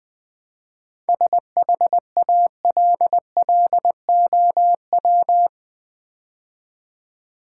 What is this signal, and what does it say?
Morse code, heard as a keyed tone, about 20 wpm, 700 Hz. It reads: SHALLOW
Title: 7 Letter Words - First 3 Letters Given - 20wpm